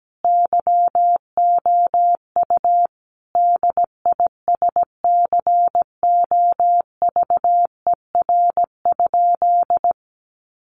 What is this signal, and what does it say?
Morse code, heard as a keyed tone, about 17 wpm, 700 Hz. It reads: YOU DISCOVER?
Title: Sentences from Top 1100 Words 17wpm